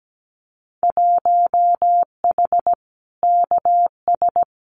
Morse code, heard as a keyed tone, about 17 wpm, 700 Hz. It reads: 1H KS
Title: Winter Field Day Exchanges 17wpm